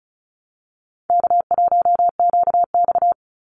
Morse code, heard as a keyed tone, about 34 wpm, 700 Hz. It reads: K1QX
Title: Easy Call Signs Repeated 2x - 35wpm